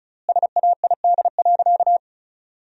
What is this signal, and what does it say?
Morse code, heard as a keyed tone, about 36 wpm, 700 Hz. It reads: SAID.
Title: 4x Word-Spacing - Sentences from Top 100 Words - 35wpm